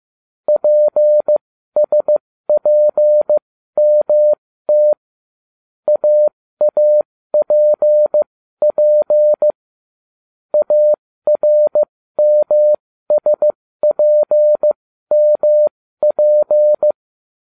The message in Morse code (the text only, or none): PSPMT AAPP ARMSPMP